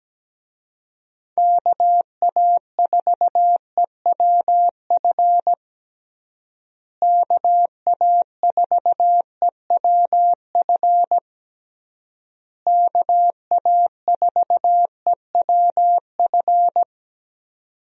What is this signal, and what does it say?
Morse code, heard as a keyed tone, about 17 wpm, 700 Hz. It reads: KA4EWF KA4EWF KA4EWF
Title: USA Call Signs - 6 Characters - Repeated 3x - 17wpm